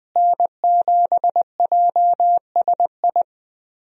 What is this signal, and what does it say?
Morse code, heard as a keyed tone, about 20 wpm, 700 Hz. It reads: N7JSI
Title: USA Call Signs - 5 Characters - Repeated 2x - 20wpm